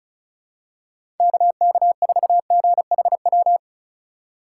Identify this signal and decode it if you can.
Morse code, heard as a keyed tone, about 36 wpm, 700 Hz. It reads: KK4GHW
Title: USA Call Signs - 4x Character Spacing - 35wpm